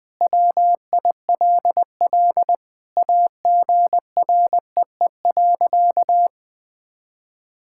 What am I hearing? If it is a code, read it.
Morse code, heard as a keyed tone, about 20 wpm, 700 Hz. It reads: WILL AGREE.
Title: Sentences with 5-Word Phrases - Spoken Beginning - 20wpm